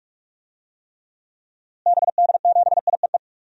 Morse code, heard as a keyed tone, about 46 wpm, 700 Hz. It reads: BD7IEE
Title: DXCC-100 Call Signs - 6 Characters - Repeated 2x - 45wpm